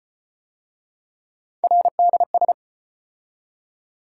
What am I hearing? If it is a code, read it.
Morse code, heard as a keyed tone, about 33 wpm, 700 Hz. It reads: RDS